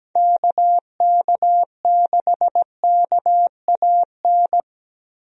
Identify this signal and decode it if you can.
Morse code, heard as a keyed tone, about 17 wpm, 700 Hz. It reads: KK6KAN
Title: USA Call Signs - 6 Characters - Repeated 2x - 17wpm